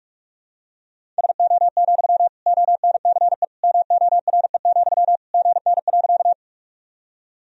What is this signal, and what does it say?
Morse code, heard as a keyed tone, about 41 wpm, 700 Hz. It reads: SO, ONCE MORE, GN.
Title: Sentences from Top 300 Words and Abbreviations 45wpm